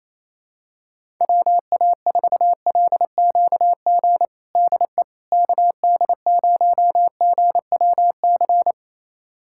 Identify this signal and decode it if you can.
Morse code, heard as a keyed tone, about 27 wpm, 700 Hz. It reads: WA4LQG DE KD0GWC
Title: USA Callsign DE Callsign - 6 Characters - 28wpm